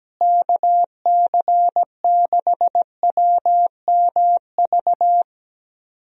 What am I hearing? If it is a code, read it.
Morse code, heard as a keyed tone, about 17 wpm, 700 Hz. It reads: KC6WMV
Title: USA Call Signs - 6 Characters - Repeated 3x - 17wpm